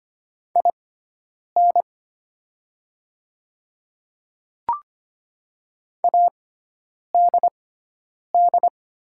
Morse code, heard as a keyed tone, about 25 wpm, spaced as for 9 wpm, 700 Hz. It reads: IN ADD
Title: 6x Character Spacing - Top 500 Words - 25wpm